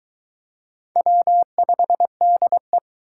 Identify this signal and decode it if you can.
Morse code, heard as a keyed tone, about 23 wpm, 700 Hz. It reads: W5DE